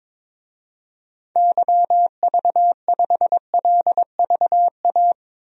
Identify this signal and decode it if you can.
Morse code, heard as a keyed tone, about 21 wpm, 700 Hz. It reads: YV5LVA
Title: DXCC-100 Call Signs - 6 Characters - 22wpm